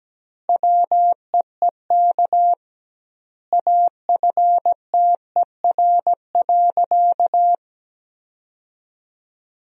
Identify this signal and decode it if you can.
Morse code, heard as a keyed tone, about 17 wpm, 700 Hz. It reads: WEEK AFTER.
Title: 2x Word-Spacing - Sentences from Top 500 Words - 17wpm